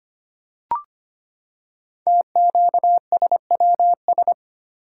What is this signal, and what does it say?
Morse code, heard as a keyed tone, about 25 wpm, 700 Hz. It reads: TQSWS
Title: Code Groups - 5 Letters - 25wpm